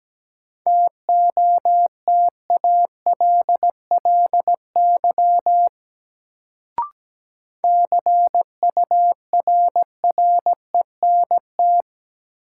Morse code, heard as a keyed tone, about 17 wpm, 700 Hz. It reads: TOTALLY CURRENT
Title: Sentences from Top 1600 Words - New Word Drill Alt - 17wpm